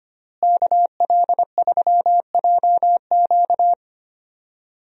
Morse code, heard as a keyed tone, about 25 wpm, 700 Hz. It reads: KL3JQ
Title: USA Call Signs - 5 Characters - Repeated 2x - 25wpm